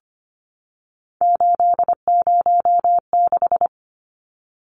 Morse code, heard as a keyed tone, about 25 wpm, 700 Hz. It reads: 806
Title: Numbers - 3-Digit 25wpm